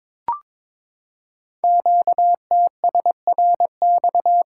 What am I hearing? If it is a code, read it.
Morse code, heard as a keyed tone, about 23 wpm, 700 Hz. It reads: QTSRX
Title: Code Groups - 5 Letters - 22wpm